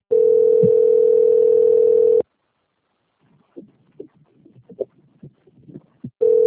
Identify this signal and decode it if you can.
Ringback tone (North America), 440 + 480 Hz.
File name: out- - - - - .73207